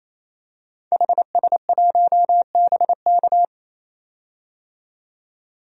Morse code, heard as a keyed tone, about 28 wpm, 700 Hz. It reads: HS1BK
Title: DXCC-100 Call Signs - 5 Characters - 28wpm